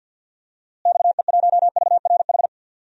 Morse code, heard as a keyed tone, about 46 wpm, 700 Hz. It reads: XE1VRH
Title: DXCC-100 Call Signs - 6 Characters - Repeated 3x - 50wpm